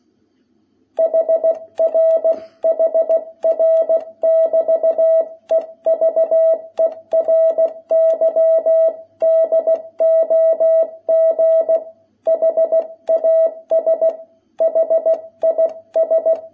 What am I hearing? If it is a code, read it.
Morse code, heard as a keyed tone, about 16 wpm, 660 Hz. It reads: HR HR<BT>EVERY DOG HAS HIS
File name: jh8cbh / 2021cw